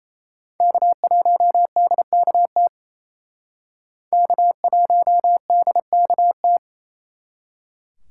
Morse code, heard as a keyed tone, about 36 wpm, 700 Hz. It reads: K1DKT K1DKT